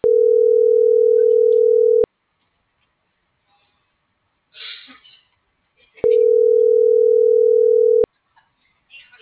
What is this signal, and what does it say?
Ringback tone (North America), 440 + 480 Hz.